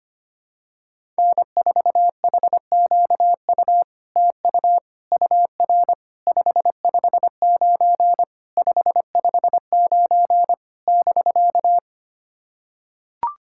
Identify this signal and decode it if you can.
Morse code, heard as a keyed tone, about 25 wpm, 700 Hz. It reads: N4HQU TU UR 559 559 <BK>
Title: SOTA QSO Practice 25wpm